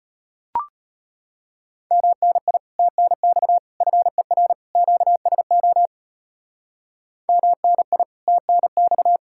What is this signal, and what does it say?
Morse code, heard as a keyed tone, about 32 wpm, 700 Hz. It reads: MNI TNX FER QSO MNI TNX